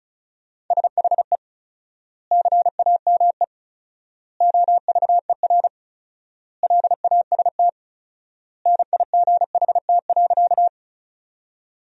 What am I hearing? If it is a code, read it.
Morse code, heard as a keyed tone, about 34 wpm, 700 Hz. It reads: SHE CAME OVER LAST NIGHT.
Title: 4x Word-Spacing - Sentences from Top 300 Words - 35wpm